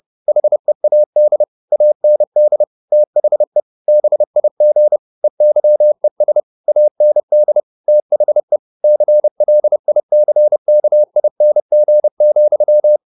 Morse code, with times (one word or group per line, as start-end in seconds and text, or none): 0.3-1.4 HEAD
1.7-2.6 AND
2.9-3.6 THE
3.9-5.0 BIG
5.2-6.4 EYES
6.7-7.6 AND
7.9-8.6 THE
8.8-13.0 CLICKING,